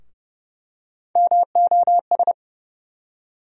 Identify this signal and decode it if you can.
Morse code, heard as a keyed tone, about 32 wpm, 700 Hz. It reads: MOS
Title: USA Call Sign Postfixes - 3 Letters 30wpm